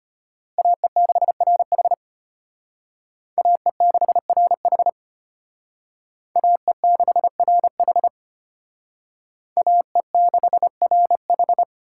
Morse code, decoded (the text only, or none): AE6RH AE6RH AE6RH AE6RH